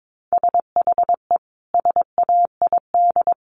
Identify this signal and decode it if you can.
Morse code, heard as a keyed tone, about 22 wpm, 700 Hz. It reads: SHE SAID
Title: Sentences from Top 1200 Words 22wpm